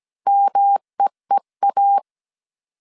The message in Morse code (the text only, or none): MEEA